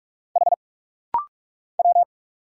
Morse code, heard as a keyed tone, about 46 wpm, 690 Hz. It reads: H W
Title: Single Letters - Rapid-Fire 45wpm